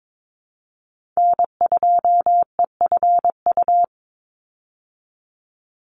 Morse code, heard as a keyed tone, about 22 wpm, 700 Hz. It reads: N2EFU
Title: USA Call Signs 22wpm